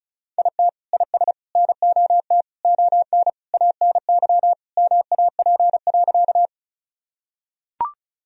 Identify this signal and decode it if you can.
Morse code, heard as a keyed tone, about 36 wpm, 700 Hz. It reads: IT IS NOT ON ANY MAP.
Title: Sentences from Top 1000 Words - Review Part 2 - 35wpm